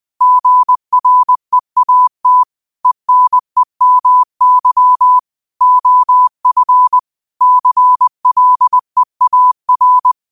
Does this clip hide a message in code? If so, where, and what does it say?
Yes: Morse code from 0.2 to 10.1 s, heard as a keyed tone, about 20 wpm, 1000 Hz. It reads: GREAT ENEMY OF CLEAR